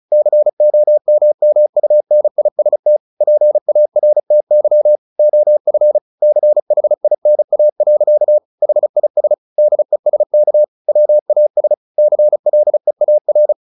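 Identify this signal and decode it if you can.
Morse code, heard as a keyed tone, about 35 wpm, 600 Hz. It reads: COMMUNIST PARTY OF CHINA. HIS DESK WAS CLEAR
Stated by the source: Morse practice at 35 wpm from CW med Gnister, sending Ken Follett's novel Never in English.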